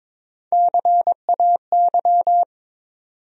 Morse code, heard as a keyed tone, about 22 wpm, 700 Hz. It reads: CAY
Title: USA Call Sign Postfixes - 3 Letters 22wpm